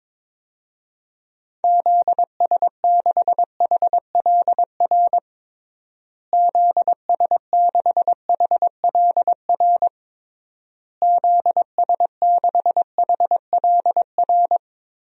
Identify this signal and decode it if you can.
Morse code, heard as a keyed tone, about 22 wpm, 700 Hz. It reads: ZS6HLR ZS6HLR ZS6HLR
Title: DXCC-100 Call Signs - 6 Characters - Repeated 3x - 22wpm